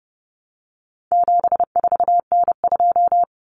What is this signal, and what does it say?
Morse code, heard as a keyed tone, about 30 wpm, 700 Hz. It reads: 74N2
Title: Common Cut Numbers - 4-Digit 30wpm